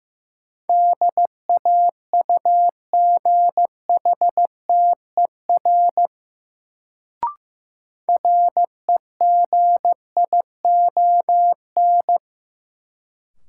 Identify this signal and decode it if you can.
Morse code, heard as a keyed tone, about 15 wpm, 700 Hz. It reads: DAUGHTER REGION
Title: Sentences from Top 1500 Words - New Word Drill Alt - 15wpm